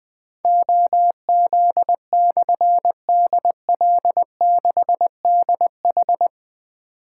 Morse code, heard as a keyed tone, about 20 wpm, 700 Hz. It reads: OZ/DL6DH
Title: Difficult Call Signs 20wpm